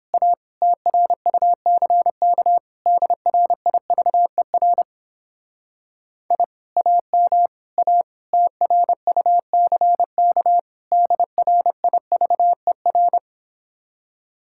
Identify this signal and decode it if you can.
Morse code, heard as a keyed tone, about 29 wpm, 700 Hz. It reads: A TRUCK DRIVER I AM A TRUCK DRIVER